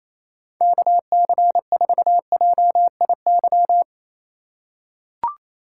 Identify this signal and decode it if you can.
Morse code, heard as a keyed tone, about 28 wpm, 700 Hz. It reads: KC4JIY
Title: USA Call Signs - 6 Characters - 3x Character Spacing - 28wpm